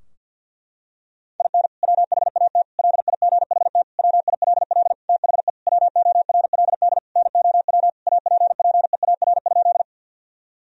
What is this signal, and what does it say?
Morse code, heard as a keyed tone, about 52 wpm, 700 Hz. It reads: IN WHAT LIGHT WILL THE WORLD NOW APPEAR?
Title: Sentences from Top 1000 Words - Review Part 2 - 50wpm